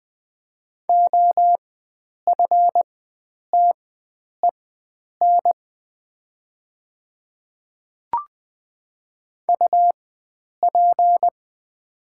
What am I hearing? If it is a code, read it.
Morse code, heard as a keyed tone, about 20 wpm, spaced as for 9 wpm, 700 Hz. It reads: OFTEN UP
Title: 4x Character Spacing - Top 300 Words - 20wpm